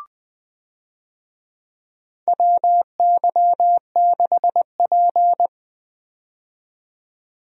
Morse code, heard as a keyed tone, about 20 wpm, 700 Hz. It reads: WY6P
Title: USA Call Signs - 4 Characters - 20wpm